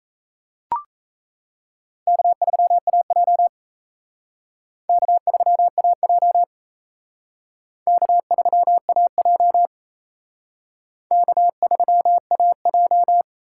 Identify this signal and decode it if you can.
Morse code, heard as a keyed tone, about 34 wpm, 700 Hz. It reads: K3AJ K3AJ K3AJ K3AJ
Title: Speed Racing - USA 1x2 Call Signs 28wpm